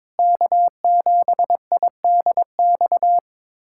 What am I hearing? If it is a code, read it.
Morse code, heard as a keyed tone, about 22 wpm, 700 Hz. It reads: K7IDX
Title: USA Callsign DE Callsign 22wpm